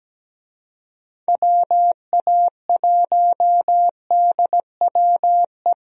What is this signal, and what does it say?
Morse code, heard as a keyed tone, about 17 wpm, 700 Hz. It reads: WA1DWE
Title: USA Call Signs - 6 Characters - Repeated 3x - 17wpm